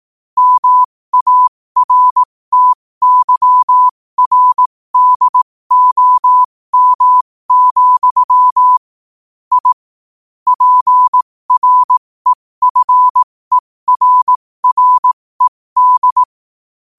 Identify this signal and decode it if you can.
Morse code, heard as a keyed tone, about 18 wpm, spaced as for 15 wpm, 1000 Hz. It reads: MARTYRDOM, I PREFERRED
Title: 15 WPM morse code quotes for Fri, 01 Mar 2024